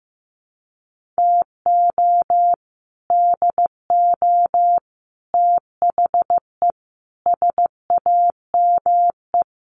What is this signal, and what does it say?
Morse code, heard as a keyed tone, about 15 wpm, 700 Hz. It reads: TO DO THE SAME
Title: Sets of 5 Words Encore 15wpm